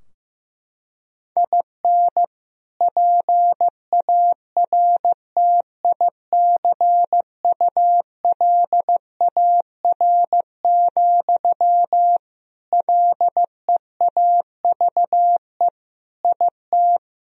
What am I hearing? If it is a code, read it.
Morse code, heard as a keyed tone, about 15 wpm, 700 Hz. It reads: IN PARTICULAR, LEAVE IT
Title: Sentences from Top 1000 Words - Review Part 4 - 15wpm